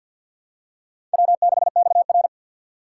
Morse code, heard as a keyed tone, about 52 wpm, 700 Hz. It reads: W6XR
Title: K1USN SST QSO Practice - 50wpm